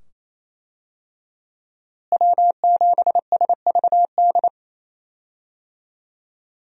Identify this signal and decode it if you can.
Morse code, heard as a keyed tone, about 28 wpm, 700 Hz. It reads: W7SVD
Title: USA Call Signs - 5 Characters - Repeated 2x - 28wpm